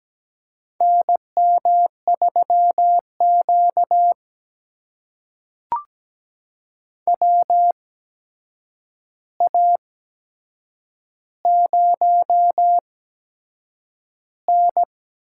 Morse code, heard as a keyed tone, about 17 wpm, spaced as for 7 wpm, 700 Hz. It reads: NM3Q WA0N
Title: USA Call Signs - 4 Characters - 8x Character Spacing - 17wpm